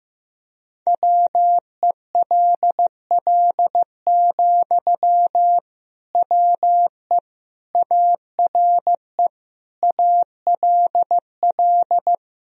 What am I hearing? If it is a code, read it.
Morse code, heard as a keyed tone, about 15 wpm, 700 Hz. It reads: WELL, WE ARE ALL
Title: Sentences from Top 1000 Words - Review Part 4 - 15wpm